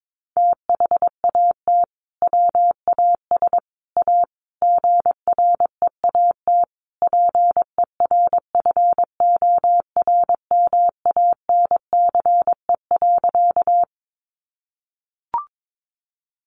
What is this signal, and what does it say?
Morse code, heard as a keyed tone, about 22 wpm, 700 Hz. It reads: THAT WAS A GREAT PERFORMANCE.
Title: Sentences from Top 1000 Words 22wpm